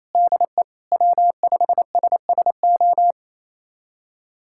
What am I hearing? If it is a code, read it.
Morse code, heard as a keyed tone, about 28 wpm, 700 Hz. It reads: DE W5SSO
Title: USA Callsign DE Callsign - 5 Characters - 28wpm